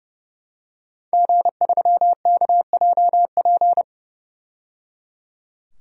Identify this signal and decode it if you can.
Morse code, heard as a keyed tone, about 30 wpm, 700 Hz. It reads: G3KJP
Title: DXCC-100 Call Signs - 5 Characters - 30wpm